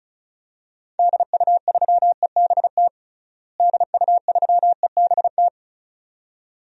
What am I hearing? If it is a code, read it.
Morse code, heard as a keyed tone, about 36 wpm, 700 Hz. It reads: DU3EBT DU3EBT
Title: DXCC-100 Call Signs - 6 Characters - Repeated 2x - 35wpm